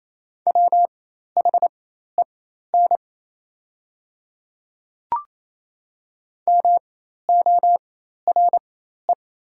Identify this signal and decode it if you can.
Morse code, heard as a keyed tone, about 28 wpm, spaced as for 13 wpm, 700 Hz. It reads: WHEN MORE
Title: 4x Character Spacing - Top 200 Words - 28wpm